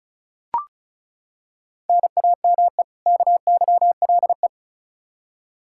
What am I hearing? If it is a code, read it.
Morse code, heard as a keyed tone, about 34 wpm, 700 Hz. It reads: NAME KYLE
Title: Practice QSO - Exchange 1 - Beginning - 35wpm